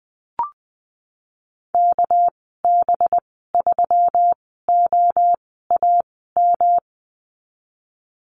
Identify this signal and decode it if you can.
Morse code, heard as a keyed tone, about 20 wpm, spaced as for 15 wpm, 700 Hz. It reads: KB3OAM
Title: 200 Lesson 32g - Letter k - Call Signs 20wpm